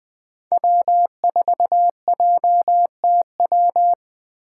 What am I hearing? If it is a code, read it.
Morse code, heard as a keyed tone, about 20 wpm, 700 Hz. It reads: W4JTW
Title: USA Call Signs - 5 Characters - 3x Character Spacing - 20wpm